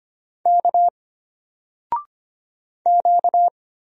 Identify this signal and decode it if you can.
Morse code, heard as a keyed tone, about 25 wpm, 700 Hz. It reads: K Q
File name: Single-Letter-Number-25wpm